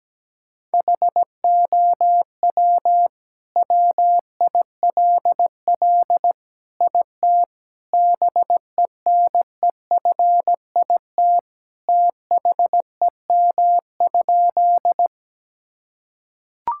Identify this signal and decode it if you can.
Morse code, heard as a keyed tone, about 17 wpm, 700 Hz. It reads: HOW WILL IT BENEFIT THEM?
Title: Sentences from Top 1000 Words - Review Part 6 - 17wpm